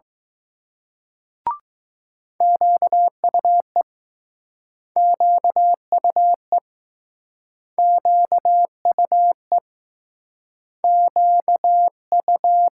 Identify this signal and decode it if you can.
Morse code, heard as a keyed tone, about 20 wpm, 700 Hz. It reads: QUE QUE QUE Q U